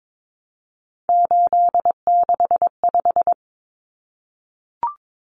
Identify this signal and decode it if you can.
Morse code, heard as a keyed tone, about 22 wpm, 700 Hz. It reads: 865
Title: Common Cut Numbers - 3-Digit 22wpm